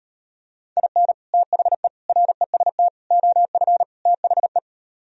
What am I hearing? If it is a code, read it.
Morse code, heard as a keyed tone, about 38 wpm, 700 Hz. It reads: IN THE REST OF THE